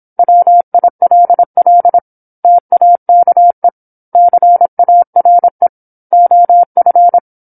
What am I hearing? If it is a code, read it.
Morse code, heard as a keyed tone, about 26 wpm, 700 Hz. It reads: WILL TAKE CARE OF
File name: 08-Chapter-08 26 WPM